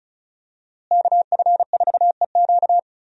Morse code, heard as a keyed tone, about 34 wpm, 700 Hz. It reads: KF4EQ
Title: USA Call Signs - 5 Characters - Repeated 2x - 35wpm